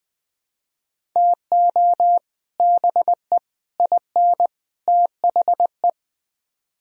Morse code, heard as a keyed tone, about 21 wpm, 700 Hz. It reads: TO BE IN THE